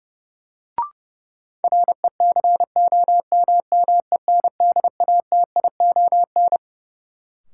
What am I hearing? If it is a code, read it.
Morse code, heard as a keyed tone, about 31 wpm, 700 Hz. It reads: RECOMMENDATION
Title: Sentences from Top 1400 Words - New Word Drill Alt - 30wpm